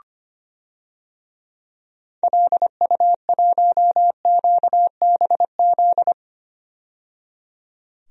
Morse code, heard as a keyed tone, about 25 wpm, 700 Hz. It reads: LU1QBZ
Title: DXCC-100 Call Signs - 6 Characters - 25wpm